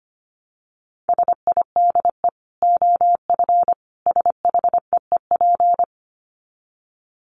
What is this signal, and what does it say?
Morse code, heard as a keyed tone, about 25 wpm, 700 Hz. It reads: SIDE OF SHEEP
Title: Dit Challenge 25wpm